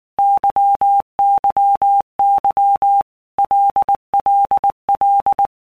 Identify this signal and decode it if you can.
Morse code, heard as a keyed tone, about 19 wpm, 800 Hz. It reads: YYY LLL
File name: Basic Letter Groups By Blocks / Groups@18 / LFCYQ@18